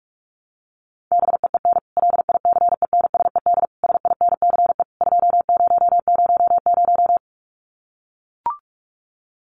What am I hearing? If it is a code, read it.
Morse code, heard as a keyed tone, about 44 wpm, 700 Hz. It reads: BEEN LICENSED SINCE 2000
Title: Common QSO Phrases 45wpm